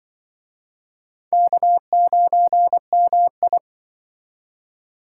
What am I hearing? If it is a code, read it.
Morse code, heard as a keyed tone, about 25 wpm, 700 Hz. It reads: K9MI
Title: Speed Racing - USA 1x2 Call Signs 20wpm